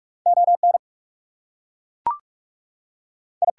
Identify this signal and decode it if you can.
Morse code, heard as a keyed tone, about 46 wpm, 700 Hz. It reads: ON I
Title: Two Letter Words - 45wpm